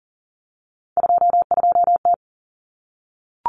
Morse code, heard as a keyed tone, about 40 wpm, 700 Hz. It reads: 22T
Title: Common Cut Numbers - 3-Digit 40wpm